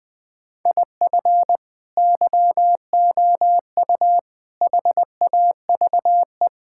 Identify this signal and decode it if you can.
Morse code, heard as a keyed tone, about 20 wpm, 700 Hz. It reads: IF YOU HAVE
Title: Build A Phrase - 4-Word Phrases - 20wpm